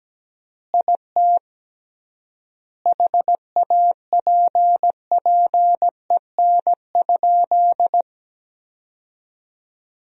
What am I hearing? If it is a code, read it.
Morse code, heard as a keyed tone, about 17 wpm, 700 Hz. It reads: IT HAPPEN?
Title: 3x Word-Spacing - Sentences from Top 400 Words - 17wpm